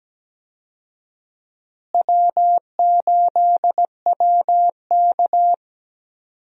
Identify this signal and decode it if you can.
Morse code, heard as a keyed tone, about 17 wpm, 700 Hz. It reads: W8WK
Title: USA Call Signs - 4 Characters - Repeated 2x - 17wpm